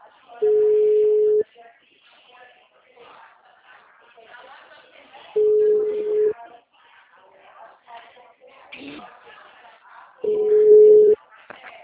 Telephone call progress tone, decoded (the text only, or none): ringback tone